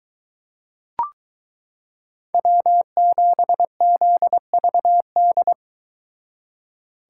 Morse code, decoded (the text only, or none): W7ZVD